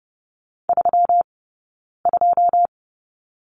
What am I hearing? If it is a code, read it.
Morse code, heard as a keyed tone, about 30 wpm, spaced as for 9 wpm, 700 Hz. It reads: 32
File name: Lesson 20d - Number 3 - Two Characters 30wpm